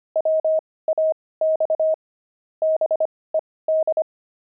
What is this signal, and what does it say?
Morse code, heard as a keyed tone, about 25 wpm, spaced as for 18 wpm, 630 Hz. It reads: WAX BED
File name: pangram-fs-18-cs-25-five